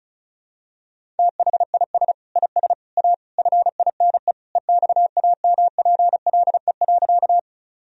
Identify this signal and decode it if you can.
Morse code, heard as a keyed tone, about 36 wpm, 700 Hz. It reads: THIS IS A FINE EXAMPLE.
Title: Sentences from Top 1000 Words - Review Part 1 - 35wpm